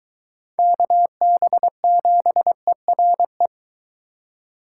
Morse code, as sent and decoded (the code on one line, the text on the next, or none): -.- -... --... .-. .
KB7ERE